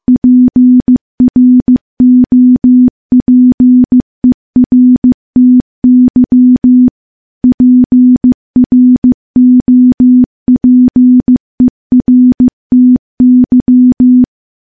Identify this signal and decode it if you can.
Morse code, heard as a keyed tone, about 15 wpm, 250 Hz. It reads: PROPERTY PROPERTY
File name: morse-training-15WPM